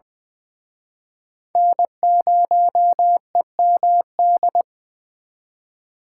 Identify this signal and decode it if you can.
Morse code, heard as a keyed tone, about 23 wpm, 700 Hz. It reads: N0EMD
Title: Speed Racing - USA 1x3 Call Signs 20wpm